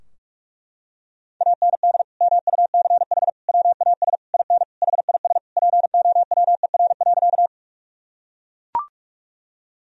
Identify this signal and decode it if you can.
Morse code, heard as a keyed tone, about 41 wpm, 700 Hz. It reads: AND MUCH WAS IN HIS POWER.
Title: Sentences from Top 1000 Words - Review Part 2 - 45wpm